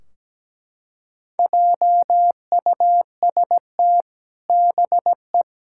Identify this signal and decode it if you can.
Morse code, heard as a keyed tone, about 17 wpm, 700 Hz. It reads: JUST BE
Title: Sentences from Top 1600 Words 17wpm